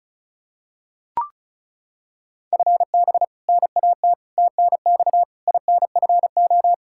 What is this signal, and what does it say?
Morse code, heard as a keyed tone, about 36 wpm, 700 Hz. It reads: FB NAT TNX INFO
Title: Practice QSO - Exchange 3 - More Info - 35wpm